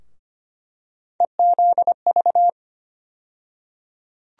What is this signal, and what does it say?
Morse code, heard as a keyed tone, about 25 wpm, 700 Hz. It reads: EZV
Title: USA Call Sign Postfixes - 3 Letters 25wpm